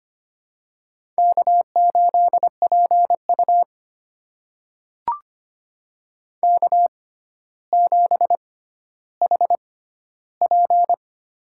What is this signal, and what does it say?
Morse code, heard as a keyed tone, about 25 wpm, spaced as for 13 wpm, 700 Hz. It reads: K8PU K7HP
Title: USA Call Signs - 4 Characters - 6x Character Spacing - 25wpm